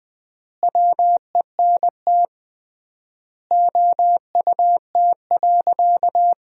Morse code, heard as a keyed tone, about 20 wpm, 700 Hz. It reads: WENT OUT.
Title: 3x Word-Spacing - Sentences from Top 300 Words - 20wpm